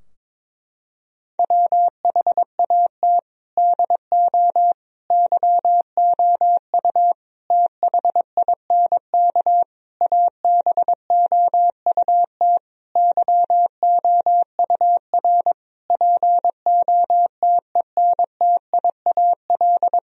Morse code, heard as a keyed tone, about 22 wpm, 700 Hz. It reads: WHAT DO YOU THINK ABOUT YOUR POTENTIAL
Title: Sentences from Top 1000 Words - Review Part 4 - 22wpm